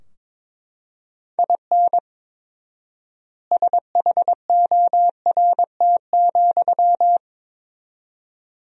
Morse code, heard as a keyed tone, about 22 wpm, 700 Hz. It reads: IN SHORT,
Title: 4x Word-Spacing - Sentences from Top 400 Words - 22wpm